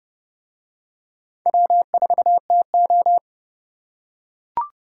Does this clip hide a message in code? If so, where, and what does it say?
Yes: Morse code from 1.5 to 3.2 s, heard as a keyed tone, about 29 wpm, 700 Hz. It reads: W4TO